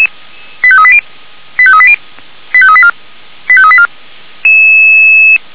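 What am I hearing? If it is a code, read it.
Five-tone selective call (ZVEI), heard as quick runs of five tones, tones of about 70 ms. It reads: 85380 85380 85384 85384
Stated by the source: ZVEI five-tone selective call received by SDR, feuerwehr-orlamuende.org (fire brigade Orlamünde) recording.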